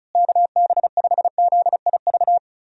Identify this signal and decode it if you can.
Morse code, heard as a keyed tone, about 36 wpm, 700 Hz. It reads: KB5ZIV
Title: USA Call Signs - 6 Characters - Repeated 3x - 35wpm